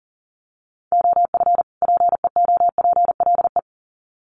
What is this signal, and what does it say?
Morse code, heard as a keyed tone, about 38 wpm, 700 Hz. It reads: OF PEOPLE
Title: Sets of 2 Words 40wpm